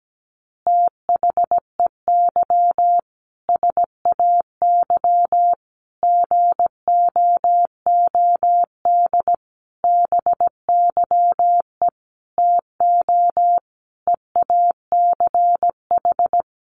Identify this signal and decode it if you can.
Morse code, heard as a keyed tone, about 17 wpm, 700 Hz. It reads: THEY SAY GOOD BYE TO EACH
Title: Easy Phrases 17wpm